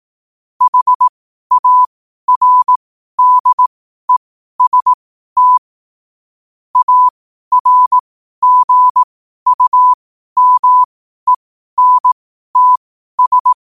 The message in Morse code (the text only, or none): HARDEST ARGUMENTS